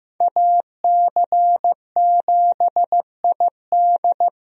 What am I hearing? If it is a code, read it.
Morse code, heard as a keyed tone, about 15 wpm, 700 Hz. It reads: AC7ID
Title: USA Call Signs - 5 Characters - Repeated 2x - 15wpm